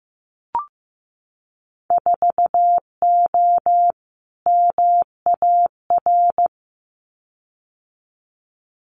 Morse code, heard as a keyed tone, about 15 wpm, 700 Hz. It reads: 4O MAR
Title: Winter Field Day Exchanges Encore 15wpm